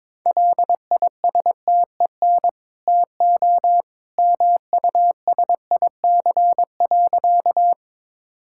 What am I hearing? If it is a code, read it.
Morse code, heard as a keyed tone, about 22 wpm, 690 Hz. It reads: LISTEN TO MUSIC.
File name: practice / 22wpm / Easy Sentences with Hints - 22wpm